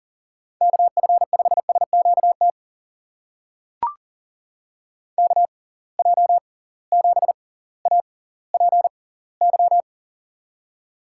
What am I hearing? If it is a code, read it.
Morse code, heard as a keyed tone, about 40 wpm, spaced as for 20 wpm, 700 Hz. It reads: KF5SQT KJ7APY
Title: USA Call Signs - 6 Characters - 6x Character Spacing - 40wpm